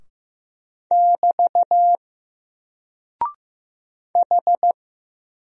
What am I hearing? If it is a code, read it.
Morse code, heard as a keyed tone, about 15 wpm, 700 Hz. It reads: <BT> H
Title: Single Character (Letter:Number:Punctuation:Prosign) 15wpm